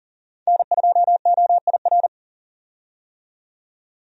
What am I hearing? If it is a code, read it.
Morse code, heard as a keyed tone, about 40 wpm, 700 Hz. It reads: N2OIR
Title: USA Call Signs - 5 Characters - 40wpm